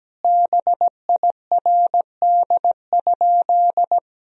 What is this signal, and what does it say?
Morse code, heard as a keyed tone, about 17 wpm, 700 Hz. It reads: BIRD?
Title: Sentences from Top 1000 Words - Review Part 2 - 17wpm